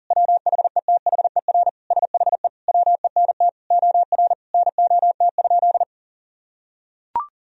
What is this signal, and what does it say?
Morse code, heard as a keyed tone, about 40 wpm, 700 Hz. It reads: WHETHER SHE WENT OR NOT?
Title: Sentences from Top 1000 Words - Review Part 3 - 40wpm